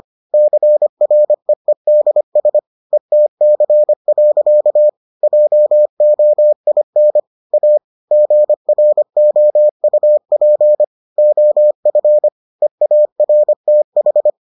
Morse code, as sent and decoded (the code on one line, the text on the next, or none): -.-. .-. . -.. ... / . - -.-. .-.-.- / .--- --- .. -. / .- / --. .-. --- ..- .--. / --- ..-. / . .- .-. - ....
CREEDS ETC. JOIN A GROUP OF EARTH